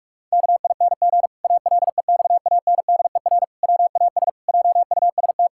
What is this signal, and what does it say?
Morse code, heard as a keyed tone, about 46 wpm, 700 Hz. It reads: KING ALEXANDER WAS JUST
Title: Pangram Challenge v2 - 45wpm